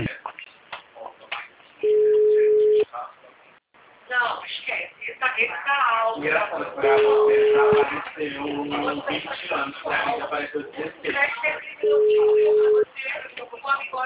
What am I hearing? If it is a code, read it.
Ringback tone (Europe, 425 Hz), with speech.
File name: out- -2021- - - .9041